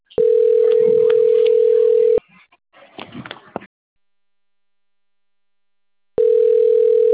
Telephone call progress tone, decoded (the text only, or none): ringback tone